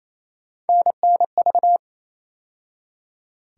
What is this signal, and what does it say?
Morse code, heard as a keyed tone, about 28 wpm, 700 Hz. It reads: NNV
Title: USA Call Sign Postfixes - 3 Letters 28wpm